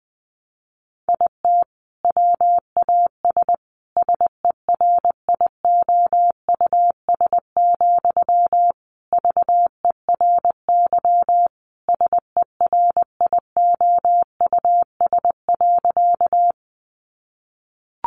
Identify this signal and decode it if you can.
Morse code, heard as a keyed tone, about 20 wpm, 700 Hz. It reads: IT WAS SERIOUS, VERY SERIOUS.
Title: Sentences from Top 1100 Words 20wpm